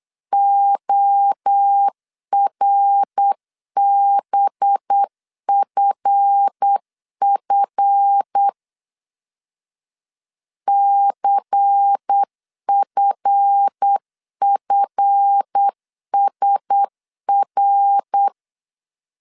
Morse code, heard as a keyed tone, about 8 wpm, 800 Hz. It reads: ORBFF CFFSR
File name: LEZIONI / 7 GF9Z / 7 GF 8wpm 3108